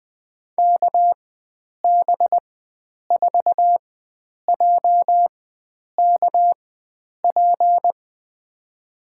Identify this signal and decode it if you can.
Morse code, heard as a keyed tone, about 20 wpm, spaced as for 9 wpm, 700 Hz. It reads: KB4JKP
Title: USA Call Signs - 6 Characters - 4x Character Spacing - 20wpm